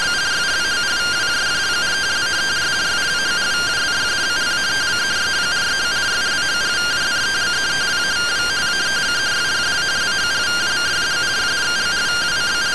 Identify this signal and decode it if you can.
Two-tone teleprinter signal, about 170 Hz shift.